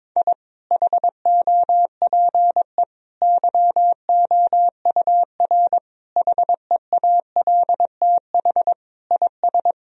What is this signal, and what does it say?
Morse code, heard as a keyed tone, about 22 wpm, 700 Hz. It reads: I HOPE YOUR HEALTH IS
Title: Sentences from Top 1000 Words - Review Part 3 - 22wpm